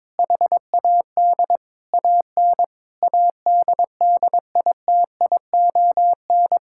Morse code, heard as a keyed tone, about 22 wpm, 700 Hz. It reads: HAD AN ADDITION
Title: Sentences from Top 1000 Words - Review Part 6 - 22wpm